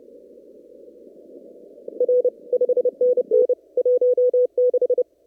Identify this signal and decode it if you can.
Morse code, heard as a keyed tone, about 30 wpm, 500 Hz. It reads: R 5NN 16
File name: CW / kt5j wpx cw-op-k5pi